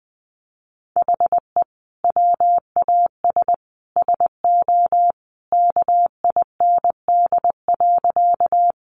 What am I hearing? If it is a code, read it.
Morse code, heard as a keyed tone, about 20 wpm, 700 Hz. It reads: HE WAS SO KIND.
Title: Sentences from Top 200 Words 20wpm